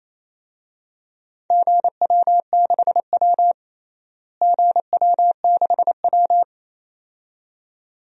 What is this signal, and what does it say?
Morse code, heard as a keyed tone, about 28 wpm, 700 Hz. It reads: GW6W GW6W
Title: DXCC-100 Call Signs - 4 Characters - Repeated 2x - 28wpm